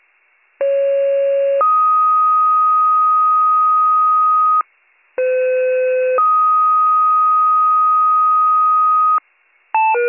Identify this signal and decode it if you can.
Two-tone page, heard as a one-second tone, then a three-second tone, about 550 then 1250 Hz, about 510 then 1250 Hz. It is heard from 0.6 to 9.2 s.